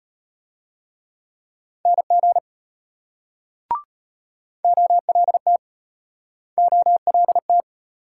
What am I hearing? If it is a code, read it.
Morse code, heard as a keyed tone, about 33 wpm, 700 Hz. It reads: NG OLT OLT